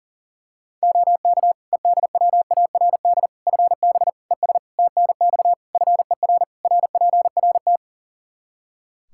Morse code, heard as a keyed tone, about 40 wpm, 700 Hz. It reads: OK EDWARD FB ES TNX FER RPRT
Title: Practice QSO - Exchange 2 - Info - 40wpm